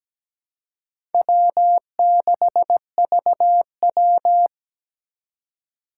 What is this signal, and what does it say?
Morse code, heard as a keyed tone, about 17 wpm, 700 Hz. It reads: W6VW
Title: USA Call Signs - 4x Character Spacing - 17wpm